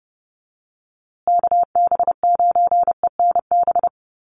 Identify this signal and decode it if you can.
Morse code, heard as a keyed tone, about 30 wpm, 700 Hz. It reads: KB9ENB
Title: USA Call Signs Repeated 3x 30wpm